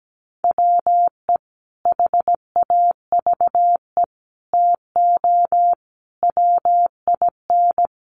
Morse code, heard as a keyed tone, about 17 wpm, 700 Hz. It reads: WE HAVE TO WIN
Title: Sentences from Top 1500 Words 17wpm